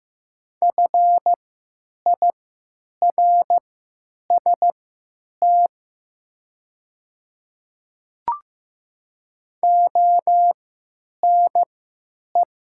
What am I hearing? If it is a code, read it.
Morse code, heard as a keyed tone, about 15 wpm, spaced as for 9 wpm, 700 Hz. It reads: FIRST ONE
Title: 3x Character Spacing - Top 100 Words - 15wpm